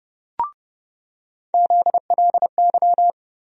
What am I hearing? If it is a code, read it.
Morse code, heard as a keyed tone, about 30 wpm, 700 Hz. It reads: ZLY
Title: USA Call Sign Postfixes - 3 Letters 30wpm